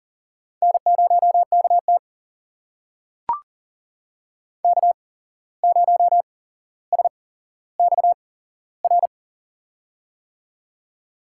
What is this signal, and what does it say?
Morse code, heard as a keyed tone, about 40 wpm, spaced as for 16 wpm, 700 Hz. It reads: N0KT K0SXR
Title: USA Call Signs - 8x Character Spacing - 40wpm